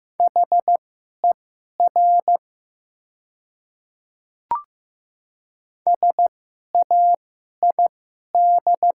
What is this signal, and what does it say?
Morse code, heard as a keyed tone, about 15 wpm, spaced as for 11 wpm, 700 Hz. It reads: HER SAID
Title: 2x Character Spacing - Top 100 Words - 15wpm